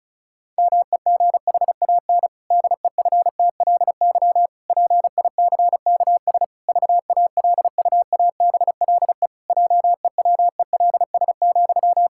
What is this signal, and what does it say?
Morse code, heard as a keyed tone, about 36 wpm, 700 Hz. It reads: MEGHAN DEFTLY PICKS VALUABLE JEWELS,
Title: Pangram Challenge v2 - 35wpm